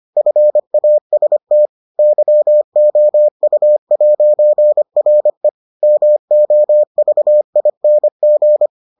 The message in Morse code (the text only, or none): FAST YOU'RE MOVING